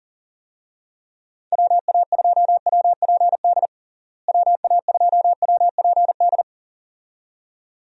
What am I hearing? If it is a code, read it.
Morse code, heard as a keyed tone, about 40 wpm, 700 Hz. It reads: WA2WPD WA2WPD
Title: USA Call Signs - 6 Characters - Repeated 2x - 40wpm